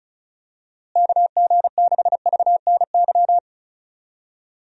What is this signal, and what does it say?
Morse code, heard as a keyed tone, about 36 wpm, 700 Hz. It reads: KG6VNY
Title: USA Call Signs - 6 Characters - Repeated 3x - 35wpm